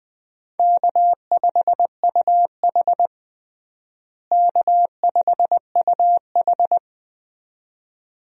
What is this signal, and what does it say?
Morse code, heard as a keyed tone, about 20 wpm, 700 Hz. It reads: K5UH K5UH
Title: USA Call Signs - 4 Characters - Repeated 2x - 20wpm